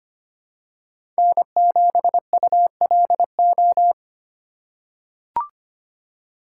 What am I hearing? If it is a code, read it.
Morse code, heard as a keyed tone, about 25 wpm, 700 Hz. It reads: N7ULO
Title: USA Call Signs - 5 Characters - 2x Character Spacing - 25wpm